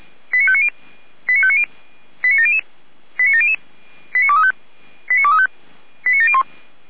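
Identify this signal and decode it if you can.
Five-tone selective call (ZVEI), heard as quick runs of five tones, tones of about 80 ms. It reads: 89590 89590 89700 89700 89235 89235 89701
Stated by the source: ZVEI five-tone selective call 89025 received by SDR.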